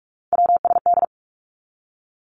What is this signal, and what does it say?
Morse code, heard as a keyed tone, about 46 wpm, 700 Hz. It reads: WSD
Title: Three Challenging Letter-Number Combinations 45wpm